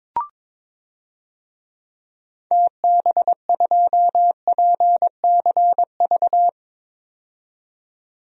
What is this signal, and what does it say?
Morse code, heard as a keyed tone, about 22 wpm, 700 Hz. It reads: TB2PCV
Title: DXCC-100 Call Signs - 6 Characters - 22wpm